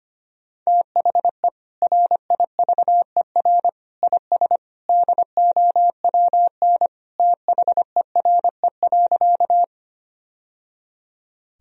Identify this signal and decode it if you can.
Morse code, heard as a keyed tone, about 25 wpm, 700 Hz. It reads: THE RIVER IS DOWN THERE.
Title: Sentences from Top 1000 Words - Review Part 1 - 25wpm